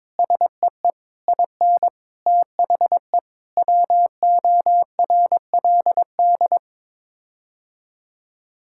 Morse code, heard as a keyed tone, about 22 wpm, 700 Hz. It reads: SEE IN THE WORLD
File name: Sets of 9 Words - 22wpm